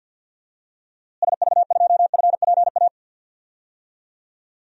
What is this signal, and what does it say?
Morse code, heard as a keyed tone, about 52 wpm, 700 Hz. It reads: SV2FPA
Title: DXCC-100 Call Signs - 6 Characters - Repeated 3x - 50wpm